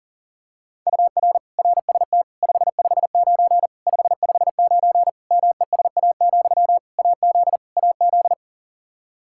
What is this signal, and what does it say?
Morse code, heard as a keyed tone, about 40 wpm, 700 Hz. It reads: UR RST 559 559 MESA, AZ AZ
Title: Practice QSO - Exchange 1 - Beginning - 40wpm